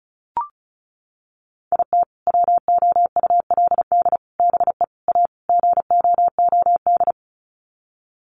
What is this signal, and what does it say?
Morse code, heard as a keyed tone, about 34 wpm, 700 Hz. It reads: IT WOULD BE A GOOD
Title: Sets of 5 Words Encore2 35wpm